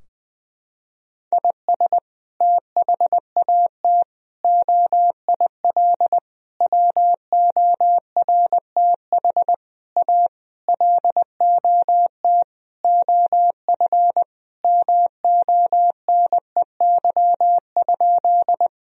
Morse code, heard as a keyed tone, about 21 wpm, 700 Hz. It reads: IS THAT OIL WORTH A LOT OF MONEY?
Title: Sentences from Top 1000 Words - Review Part 4 - 20wpm